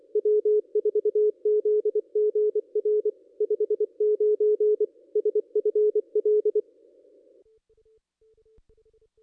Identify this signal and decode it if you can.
Morse code, heard as a keyed tone, about 24 wpm, 440 Hz. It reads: W4ZGR 59 SFL
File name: contests / ss-cw-k5tr-op-n5rz -1400